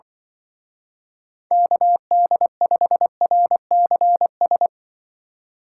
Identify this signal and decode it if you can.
Morse code, heard as a keyed tone, about 24 wpm, 700 Hz. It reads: KD5RCS
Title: Speed Racing - USA 2x3 Call Signs 20wpm